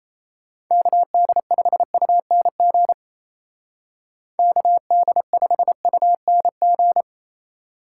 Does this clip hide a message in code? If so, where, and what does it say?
Yes: Morse code from 0.7 to 7.0 s, heard as a keyed tone, about 37 wpm, 700 Hz. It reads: KD5UNG KD5UNG